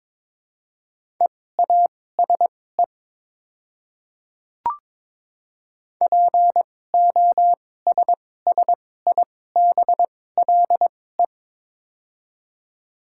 Morse code, heard as a keyed tone, about 22 wpm, spaced as for 16 wpm, 700 Hz. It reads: EASE POSSIBLE
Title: 2x Character Spacing - Top 500 Words - 22wpm